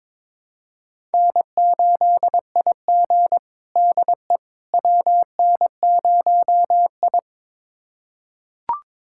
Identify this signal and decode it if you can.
Morse code, heard as a keyed tone, about 22 wpm, 700 Hz. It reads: N8IG DE WN0I
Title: USA Callsign DE Callsign - 4 Characters - 22wpm